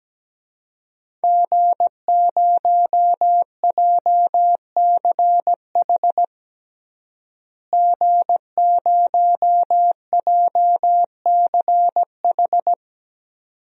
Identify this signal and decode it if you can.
Morse code, heard as a keyed tone, about 17 wpm, 700 Hz. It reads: G0JCH G0JCH
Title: DXCC-100 Call Signs - 5 Characters - Repeated 2x - 17wpm